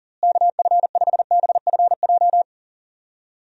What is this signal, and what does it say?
Morse code, heard as a keyed tone, about 40 wpm, 700 Hz. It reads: KF5BFJ
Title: USA Call Signs - 6 Characters - Repeated 3x - 40wpm